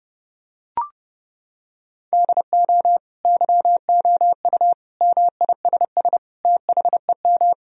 Morse code, heard as a keyed tone, about 30 wpm, 700 Hz. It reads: DO YOU MISS THEM
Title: Sentences from Top 1000 Words - Review Part 3 - 30wpm